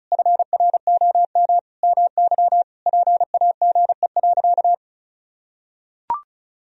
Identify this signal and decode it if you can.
Morse code, heard as a keyed tone, about 36 wpm, 700 Hz. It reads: FROM MY PAGE.
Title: Sentences from Top 1000 Words - Review Part 1 - 35wpm